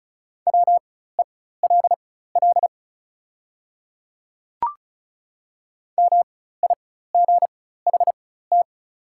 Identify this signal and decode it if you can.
Morse code, heard as a keyed tone, about 34 wpm, spaced as for 16 wpm, 690 Hz. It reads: WELL MIGHT
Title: 4x Character Spacing - Top 300 Words - 35wpm